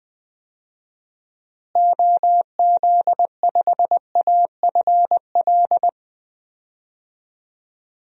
Morse code, heard as a keyed tone, about 20 wpm, 700 Hz. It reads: OZ5AFL
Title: DXCC-100 Call Signs - 6 Characters - Repeated 3x - 20wpm